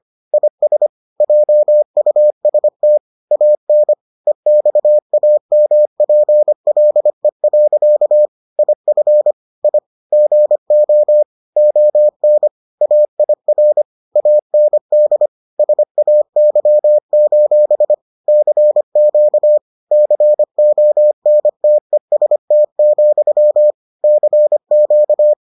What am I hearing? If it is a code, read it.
Morse code, heard as a keyed tone, about 25 wpm, 600 Hz. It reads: IS JUST AN EXAMPLE. IF I GO ON AIR AND SAY: CQ CONTEST, CQ